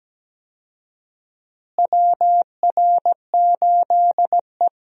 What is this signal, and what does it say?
Morse code, heard as a keyed tone, about 17 wpm, 700 Hz. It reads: WR8E
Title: USA Call Signs - 4 Characters - Repeated 3x - 17wpm